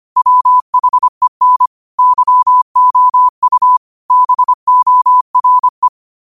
Morse code, heard as a keyed tone, about 25 wpm, 1000 Hz. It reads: WHEN YOU BORE
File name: 25wpm